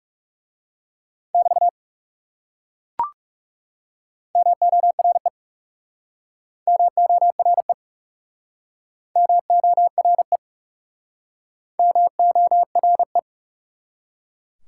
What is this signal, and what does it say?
Morse code, heard as a keyed tone, about 39 wpm, 700 Hz. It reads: <BT> MORE MORE MORE MORE